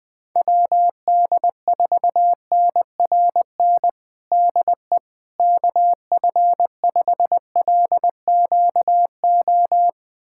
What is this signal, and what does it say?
Morse code, heard as a keyed tone, about 20 wpm, 700 Hz. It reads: WD4NRN DE KF5LQO
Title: USA Callsign DE Callsign - 6 Characters - 20wpm